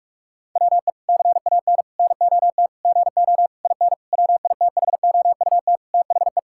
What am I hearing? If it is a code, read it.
Morse code, heard as a keyed tone, about 46 wpm, 700 Hz. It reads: WE CAN NOT GO IN WITHOUT THE
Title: Sentences from Top 1000 Words - Review Part 6 - 45wpm